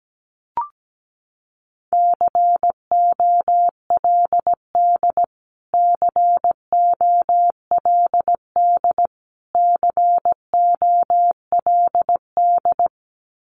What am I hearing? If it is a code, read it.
Morse code, heard as a keyed tone, about 17 wpm, 700 Hz. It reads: COLD COLD COLD
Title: Top 500 Words Repeated 17wpm